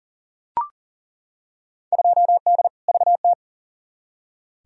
Morse code, heard as a keyed tone, about 40 wpm, 700 Hz. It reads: 2D VT
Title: Field Day Exchanges Encore - 40wpm